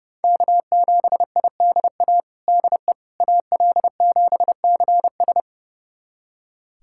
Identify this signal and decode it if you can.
Morse code, heard as a keyed tone, about 30 wpm, 700 Hz. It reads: K7IDA DE AL7CS
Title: USA Callsign DE Callsign - 5 Characters - 30wpm